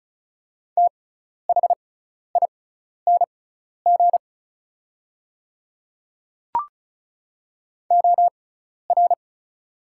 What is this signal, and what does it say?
Morse code, heard as a keyed tone, about 34 wpm, spaced as for 12 wpm, 700 Hz. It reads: THING OR
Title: 6x Character Spacing - Top 100 Words - 35wpm